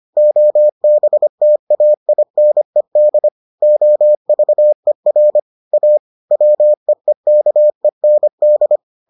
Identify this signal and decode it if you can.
Morse code, heard as a keyed tone, about 25 wpm, 600 Hz. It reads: OBTAINED OVER A WEEKEND